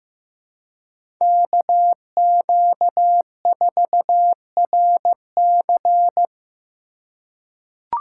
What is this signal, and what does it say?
Morse code, heard as a keyed tone, about 15 wpm, 700 Hz. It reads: KQ4RC
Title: USA Call Signs - 5 Characters - 3x Character Spacing - 15wpm